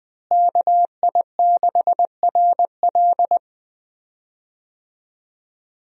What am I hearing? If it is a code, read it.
Morse code, heard as a keyed tone, about 20 wpm, 700 Hz. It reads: KI6RL
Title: USA Call Signs - 5 Characters - Repeated 2x - 20wpm